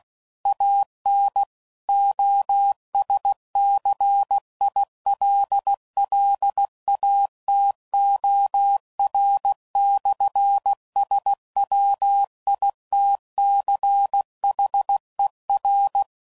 Morse code, heard as a keyed tone, about 16 wpm, 800 Hz. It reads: AN OSCILLATOR/SWITCHER
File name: CW Code Groups @15WPM / Technical Subjects / PwrSupply